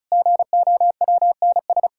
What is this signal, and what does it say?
Morse code, heard as a keyed tone, about 34 wpm, 700 Hz. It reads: GOWNS
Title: Pangram Challenge v2 - 35wpm